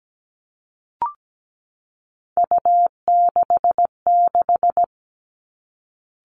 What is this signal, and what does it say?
Morse code, heard as a keyed tone, about 17 wpm, 700 Hz. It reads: U66
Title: Three Challenging Letter-Number Combinations 17wpm